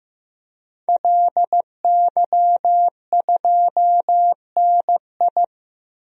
Morse code, heard as a keyed tone, about 15 wpm, 700 Hz. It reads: LY2NI
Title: DXCC-100 Call Signs - 5 Characters - 15wpm